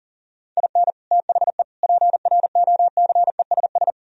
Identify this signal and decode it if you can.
Morse code, heard as a keyed tone, about 53 wpm, 700 Hz. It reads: IN THE PROCESS